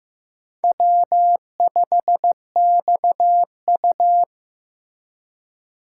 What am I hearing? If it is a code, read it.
Morse code, heard as a keyed tone, about 15 wpm, 700 Hz. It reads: W5XU
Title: K1USN SST QSO Practice - 15wpm